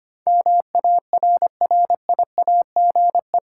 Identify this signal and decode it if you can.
Morse code, heard as a keyed tone, about 25 wpm, 700 Hz. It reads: MARRIAGE
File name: Eight Letter Words - 25wpm